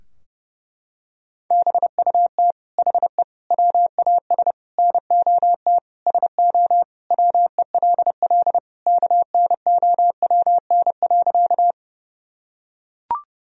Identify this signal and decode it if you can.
Morse code, heard as a keyed tone, about 32 wpm, 700 Hz. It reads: BUT HE WAS NOT SO WELL KNOWN.
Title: Sentences from Top 1000 Words - Review Part 3 - 30wpm